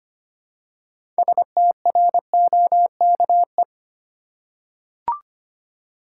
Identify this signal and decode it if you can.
Morse code, heard as a keyed tone, about 25 wpm, 700 Hz. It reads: STROKE
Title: Six Letter Words - 25wpm